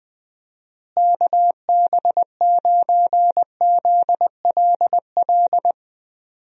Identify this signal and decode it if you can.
Morse code, heard as a keyed tone, about 21 wpm, 700 Hz. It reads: KB9ZLL
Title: USA Call Signs - 6 Characters - Repeated 3x - 20wpm